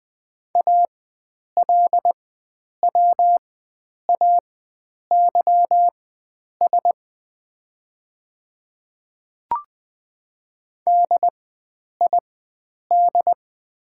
Morse code, heard as a keyed tone, about 20 wpm, spaced as for 9 wpm, 700 Hz. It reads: ALWAYS DID
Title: 4x Character Spacing - Top 500 Words - 20wpm